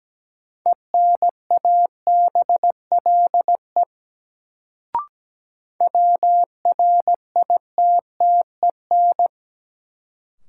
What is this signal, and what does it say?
Morse code, heard as a keyed tone, about 17 wpm, 700 Hz. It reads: ENABLE WRITTEN
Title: Sentences from Top 1500 Words - New Word Drill Alt - 17wpm